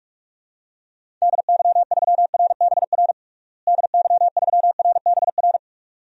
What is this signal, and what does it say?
Morse code, heard as a keyed tone, about 46 wpm, 700 Hz. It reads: DY3RBR DY3RBR
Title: DXCC-100 Call Signs - 6 Characters - Repeated 2x - 45wpm